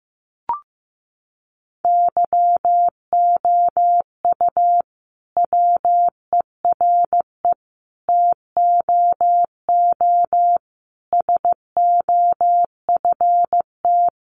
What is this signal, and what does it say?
Morse code, heard as a keyed tone, about 15 wpm, 700 Hz. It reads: YOU WERE TOO SOFT
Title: Sentences from Top 1500 Words 15wpm